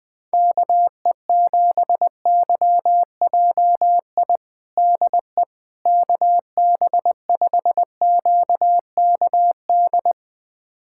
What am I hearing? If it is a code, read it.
Morse code, heard as a keyed tone, about 20 wpm, 700 Hz. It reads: KE7YJI DE KB5QKD
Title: USA Callsign DE Callsign - 6 Characters - 20wpm